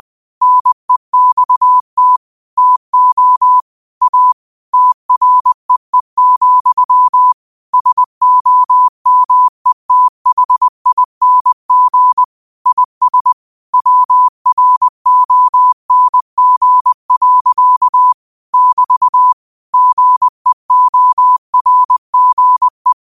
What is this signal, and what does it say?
Morse code, heard as a keyed tone, about 20 wpm, 1000 Hz. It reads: NEXT TO A TREE, SOMETHING IS WRONG. <BT> GEORGE
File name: qotdfiles / 20wpm